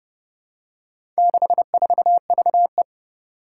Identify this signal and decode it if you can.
Morse code, heard as a keyed tone, about 32 wpm, 700 Hz. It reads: 64VE
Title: Four Challenging Letter-Number Combinations 30wpm